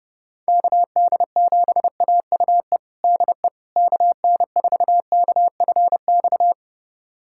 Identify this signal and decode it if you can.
Morse code, heard as a keyed tone, about 30 wpm, 700 Hz. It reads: KD7AUE DE KN4KFX
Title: USA Callsign DE Callsign - 6 Characters - 30wpm